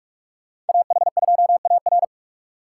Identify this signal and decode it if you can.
Morse code, heard as a keyed tone, about 57 wpm, 700 Hz. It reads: AH2AR